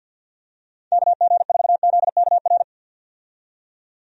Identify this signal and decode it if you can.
Morse code, heard as a keyed tone, about 48 wpm, 700 Hz. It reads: KG4ZKR
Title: USA Call Signs - 6 Characters - 50wpm